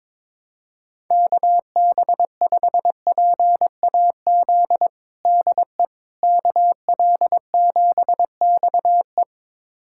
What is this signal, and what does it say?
Morse code, heard as a keyed tone, about 22 wpm, 700 Hz. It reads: KB5PAZ DE KL7XE
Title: USA Callsign DE Callsign 22wpm